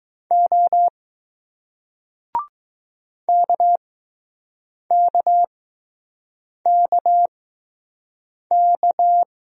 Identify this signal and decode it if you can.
Morse code, heard as a keyed tone, about 20 wpm, spaced as for 6 wpm, 700 Hz. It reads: O KKKK